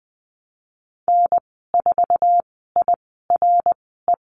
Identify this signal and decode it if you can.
Morse code, heard as a keyed tone, about 19 wpm, spaced as for 14 wpm, 700 Hz. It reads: N4IRE
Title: 055 Lesson 10g - Letter r - Call Signs 20wpm